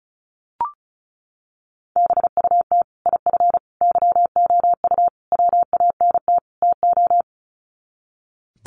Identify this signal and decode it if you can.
Morse code, heard as a keyed tone, about 34 wpm, 700 Hz. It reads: BUT IF YOU WANT TO
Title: Sets of 5 Words Encore2 35wpm